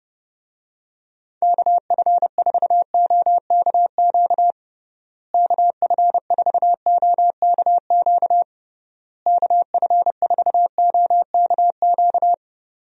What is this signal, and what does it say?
Morse code, heard as a keyed tone, about 30 wpm, 700 Hz. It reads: KF4OKQ KF4OKQ KF4OKQ
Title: USA Call Signs - 6 Characters - Repeated 3x - 30wpm